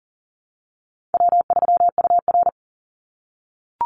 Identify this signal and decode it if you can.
Morse code, heard as a keyed tone, about 41 wpm, 700 Hz. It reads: W3UR
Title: Easy Call Signs - 40wpm